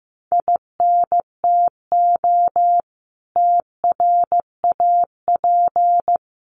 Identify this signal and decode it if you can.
Morse code, heard as a keyed tone, about 15 wpm, 700 Hz. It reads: INTO TRAP
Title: Common QSO Phrases 15wpm